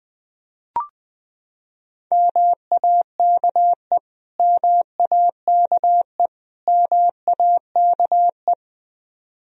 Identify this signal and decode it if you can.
Morse code, heard as a keyed tone, about 20 wpm, 700 Hz. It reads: MAKE MAKE MAKE
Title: Top 100 Words Repeated 20wpm